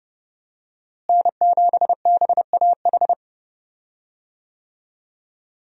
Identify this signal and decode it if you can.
Morse code, heard as a keyed tone, about 30 wpm, 700 Hz. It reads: N7BAH
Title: USA Call Signs - 5 Characters - Repeated 2x - 30wpm